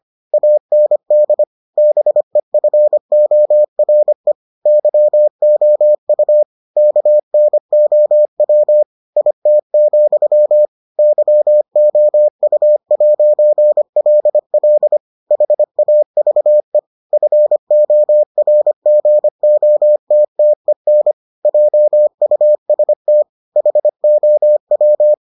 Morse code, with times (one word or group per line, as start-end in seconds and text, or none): 0.3-1.4 AND
1.8-4.3 BEFORE
4.7-6.4 YOU
6.8-8.8 KNOW
9.2-10.7 IT,
11.0-15.0 YOU'LL
15.3-16.8 HAVE
17.1-21.1 FORGOTTEN
21.5-23.2 JUST
23.6-25.1 HOW